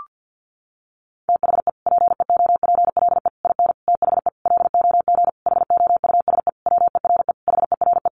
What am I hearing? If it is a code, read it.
Morse code, heard as a keyed tone, about 50 wpm, 700 Hz. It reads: THE PEOPLE IN THE LOG HOUSE WERE HERE
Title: Sentences from Top 1400 Words 50wpm